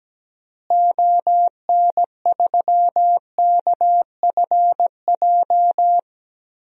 Morse code, heard as a keyed tone, about 17 wpm, 700 Hz. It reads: ON3KFJ
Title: DXCC-100 Call Signs - 6 Characters - Repeated 3x - 17wpm